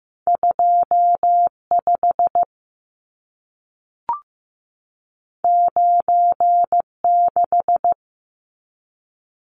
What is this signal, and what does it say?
Morse code, heard as a keyed tone, about 15 wpm, 700 Hz. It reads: 25 96
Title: Numbers - 2-Digit 15wpm